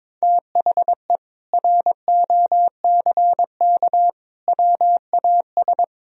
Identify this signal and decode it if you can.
Morse code, heard as a keyed tone, about 22 wpm, 700 Hz. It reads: THE ROCK WAS
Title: Sentences from Top 400 Words and Abbreviations 22wpm